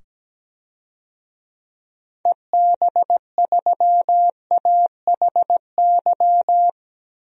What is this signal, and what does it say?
Morse code, heard as a keyed tone, about 17 wpm, 700 Hz. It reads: EB3AHY
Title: DXCC-100 Call Signs - 6 Characters - 17wpm